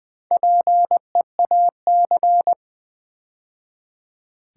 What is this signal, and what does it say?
Morse code, heard as a keyed tone, about 20 wpm, 700 Hz. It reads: PEAC
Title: Build A Word - 2 Letters At A Time - 8-Letter Words - 20wpm